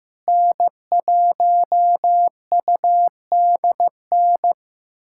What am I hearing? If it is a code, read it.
Morse code, heard as a keyed tone, about 15 wpm, 700 Hz. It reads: N1UDN
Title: USA Callsign DE Callsign 15wpm